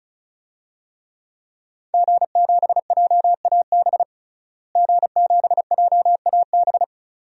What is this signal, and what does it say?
Morse code, heard as a keyed tone, about 36 wpm, 700 Hz. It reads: G7JAB G7JAB
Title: DXCC-100 Call Signs - 5 Characters - Repeated 2x - 35wpm